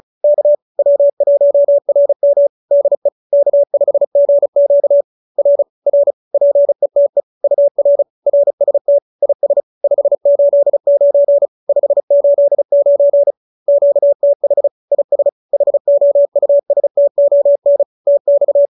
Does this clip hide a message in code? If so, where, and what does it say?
Yes: Morse code from 0.2 to 18.7 s, heard as a keyed tone, about 34 wpm, 570 Hz. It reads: K W1RM DE K5GQ R R PETE UR RST IS 589 589 QTH IS HOUSTON TX